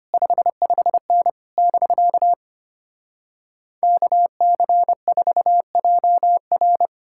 Morse code, heard as a keyed tone, about 33 wpm, 700 Hz. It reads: 55N <BK> KC4JR